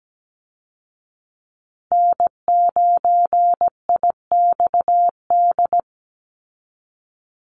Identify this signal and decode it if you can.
Morse code, heard as a keyed tone, about 17 wpm, 700 Hz. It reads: N9IXD
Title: Call Signs Repeated 2x - 17wpm